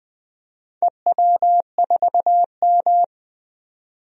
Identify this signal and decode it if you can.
Morse code, heard as a keyed tone, about 20 wpm, 700 Hz. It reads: EW4M
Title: DXCC-100 Call Signs - 4 Characters - 20wpm